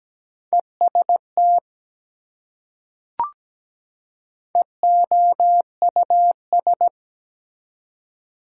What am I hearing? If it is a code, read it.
Morse code, heard as a keyed tone, about 17 wpm, 700 Hz. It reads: EST EOUS
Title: Prefixes and Suffixes 17wpm